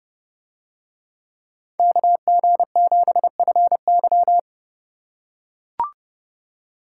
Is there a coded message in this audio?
Yes: Morse code from 1.8 to 4.4 s, heard as a keyed tone, about 27 wpm, 700 Hz. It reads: KG7FY